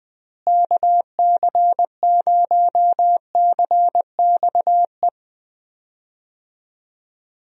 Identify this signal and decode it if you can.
Morse code, heard as a keyed tone, about 20 wpm, 700 Hz. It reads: KC0CXE
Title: USA Call Signs - 6 Characters - Repeated 2x - 20wpm